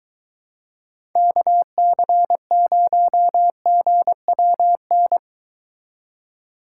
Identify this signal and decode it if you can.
Morse code, heard as a keyed tone, about 22 wpm, 700 Hz. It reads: KC0GWN